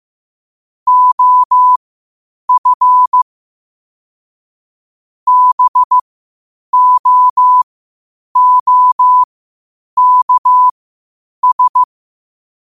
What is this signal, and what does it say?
Morse code, heard as a keyed tone, about 15 wpm, spaced as for 8 wpm, 1000 Hz. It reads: OF BOOKS
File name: qotdfiles / 8wpm